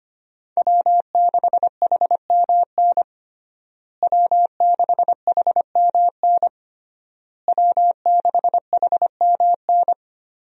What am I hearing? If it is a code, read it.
Morse code, heard as a keyed tone, about 25 wpm, 700 Hz. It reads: W6HMN W6HMN W6HMN
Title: USA Call Signs - 5 Characters - Repeated 3x - 25wpm